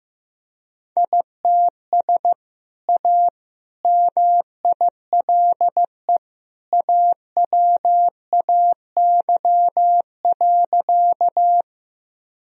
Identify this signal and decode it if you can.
Morse code, heard as a keyed tone, about 15 wpm, 700 Hz. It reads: ITS A MILE AWAY.
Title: Sentences from Top 1000 Words - Review Part 1 - 15wpm